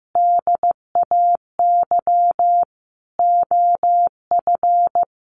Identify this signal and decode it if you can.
Morse code, heard as a keyed tone, about 15 wpm, 700 Hz. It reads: DAY OF
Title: Sentences from Top 1100 Words 15wpm